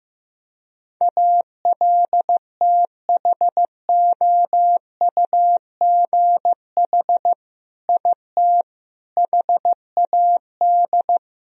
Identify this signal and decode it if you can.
Morse code, heard as a keyed tone, about 15 wpm, 700 Hz. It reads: ALTHOUGH IT HAD
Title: Sentences from Top 1000 Words - Review Part 3 - 15wpm